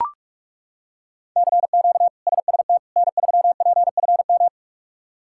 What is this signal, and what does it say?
Morse code, heard as a keyed tone, about 41 wpm, 700 Hz. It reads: CQ SST N3PFM
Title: K1USN SST QSO Practice - 45wpm